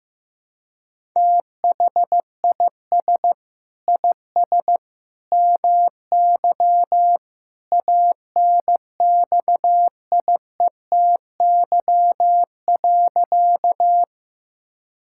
Morse code, heard as a keyed tone, about 16 wpm, 700 Hz. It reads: THIS IS MY ANXIETY.
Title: Sentences from Top 1600 Words 15wpm